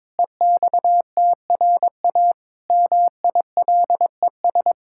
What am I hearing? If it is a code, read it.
Morse code, heard as a keyed tone, about 22 wpm, 700 Hz. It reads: EXTRA MILES
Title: Pangram Challenge v2 - 22wpm